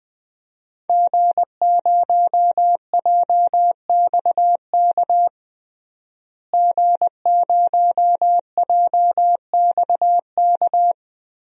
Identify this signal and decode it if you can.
Morse code, heard as a keyed tone, about 20 wpm, 700 Hz. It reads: G0JXK G0JXK
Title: DXCC-100 Call Signs - 5 Characters - Repeated 2x - 20wpm